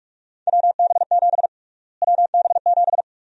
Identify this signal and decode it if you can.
Morse code, heard as a keyed tone, about 48 wpm, 700 Hz. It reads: WB7 WB7
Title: USA Call Sign Prefixes - Repeated 2x - 45wpm